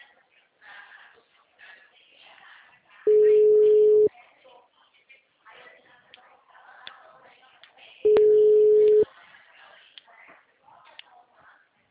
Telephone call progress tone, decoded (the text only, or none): ringback tone